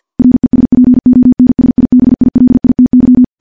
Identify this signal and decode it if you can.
Morse code, heard as a keyed tone, about 50 wpm, 250 Hz. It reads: RESPONSIBILITY
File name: morse / morse-training-50WPM